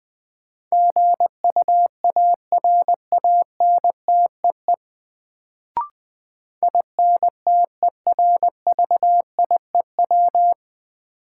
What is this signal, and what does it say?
Morse code, heard as a keyed tone, about 20 wpm, 700 Hz. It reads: GUARANTEE INTERVIEW
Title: Sentences from Top 1600 Words - New Word Drill - 20wpm